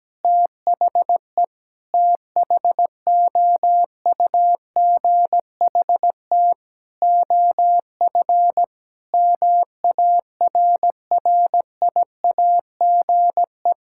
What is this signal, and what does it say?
Morse code, heard as a keyed tone, about 17 wpm, 700 Hz. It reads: THE THOUGHT OF MARRIAGE
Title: Sentences from Top 1000 Words - Review Part 5 - 17wpm